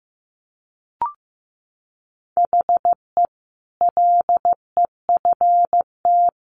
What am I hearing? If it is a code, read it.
Morse code, heard as a keyed tone, about 15 wpm, 700 Hz. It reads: HE LEFT
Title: Sentences from Top 1100 Words 15wpm